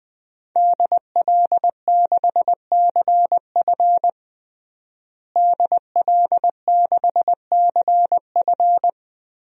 Morse code, heard as a keyed tone, about 21 wpm, 700 Hz. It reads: DL6CF DL6CF
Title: DXCC-100 Call Signs - 5 Characters - Repeated 2x - 20wpm